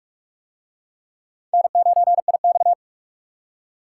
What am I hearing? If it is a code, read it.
Morse code, heard as a keyed tone, about 46 wpm, 700 Hz. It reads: N9IX